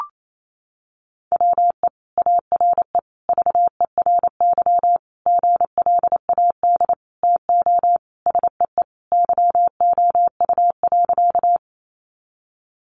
Morse code, heard as a keyed tone, about 28 wpm, 700 Hz. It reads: WE ARE VERY GLAD TO SEE YOU.
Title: Sentences from Top 800 Words 28wpm